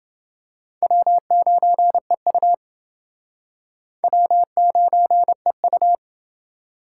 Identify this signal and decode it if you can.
Morse code, heard as a keyed tone, about 28 wpm, 700 Hz. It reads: W9EU W9EU